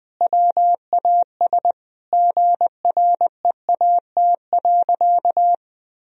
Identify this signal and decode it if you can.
Morse code, heard as a keyed tone, about 21 wpm, 700 Hz. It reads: WAS GREAT.
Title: Sentences from Top 1000 Words - Review Part 4 - 20wpm